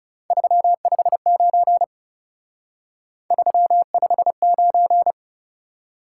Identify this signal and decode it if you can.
Morse code, heard as a keyed tone, about 38 wpm, 700 Hz. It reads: 359 359
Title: Speed-Racing Top 180 QSO Elements 30wpm